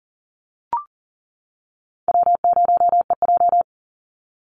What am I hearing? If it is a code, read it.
Morse code, heard as a keyed tone, about 40 wpm, 700 Hz. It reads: W0EJ
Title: Easy Call Signs - 40wpm